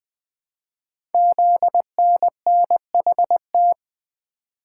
Morse code, heard as a keyed tone, about 20 wpm, 700 Hz. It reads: ZNNHT
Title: Code Groups - 5 Letters - 20wpm